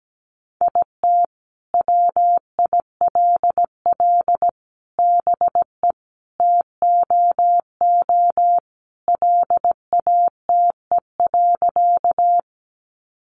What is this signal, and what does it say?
Morse code, heard as a keyed tone, about 17 wpm, 700 Hz. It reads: IT WILL BE TOO LATE.
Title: Sentences from Top 300 Words 17wpm